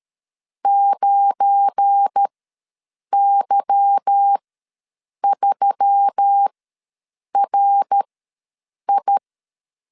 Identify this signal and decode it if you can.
Morse code, heard as a keyed tone, about 13 wpm, spaced as for 7 wpm, 800 Hz. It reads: 9Y3RI
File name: LEZIONI / 8 YJ45 / 8 Y ws 12wpm 3409